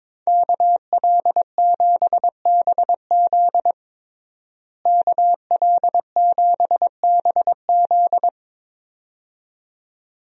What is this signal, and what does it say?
Morse code, heard as a keyed tone, about 22 wpm, 700 Hz. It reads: KL7BZ KL7BZ
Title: DXCC-100 Call Signs - 5 Characters - Repeated 2x - 22wpm